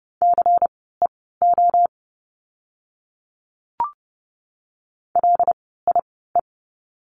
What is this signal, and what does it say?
Morse code, heard as a keyed tone, about 30 wpm, spaced as for 17 wpm, 700 Hz. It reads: CEO LIE
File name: Lesson 16f - Letter c - Words 30wpm